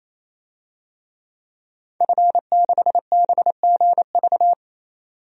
Morse code, heard as a keyed tone, about 28 wpm, 700 Hz. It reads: F6BGV
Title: DXCC-100 Call Signs - 5 Characters - 28wpm